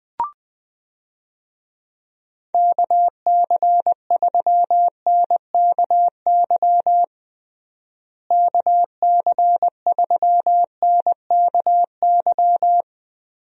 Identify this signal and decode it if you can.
Morse code, heard as a keyed tone, about 20 wpm, 700 Hz. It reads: KC3NKY KC3NKY
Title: USA Call Signs - 6 Characters - Repeated 3x - 20wpm